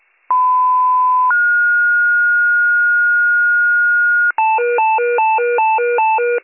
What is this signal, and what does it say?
Two-tone page, heard as a one-second tone, then a three-second tone, about 1010 then 1430 Hz. It is heard from 0.3 to 4.3 s.